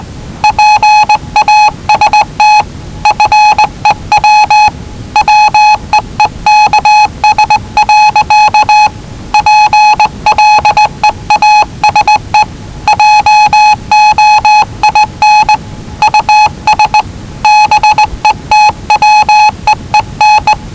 Morse code, heard as a keyed tone, about 20 wpm, 850 Hz. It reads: PAST FEW WEEKS. PLEASE JOIN US BETWEEN
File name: WellingtonRadioZlwCloseDownOnHfAndMf / 01-Zlw 4277khz 417.5 qrt until 0925z